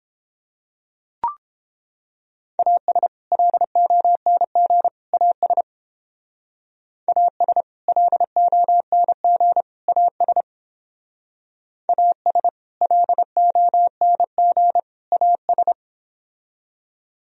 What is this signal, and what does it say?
Morse code, heard as a keyed tone, about 29 wpm, 700 Hz. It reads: AS LONG AS AS LONG AS AS LONG AS